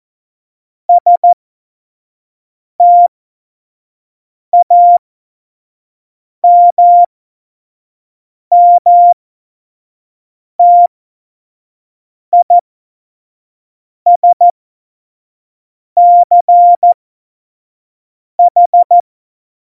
Morse code, heard as a keyed tone, about 14 wpm, spaced as for 5 wpm, 700 Hz. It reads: STAMMTISCH
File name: Morsekurs / Witze / Witz 17.Dezember 2019, 25 F70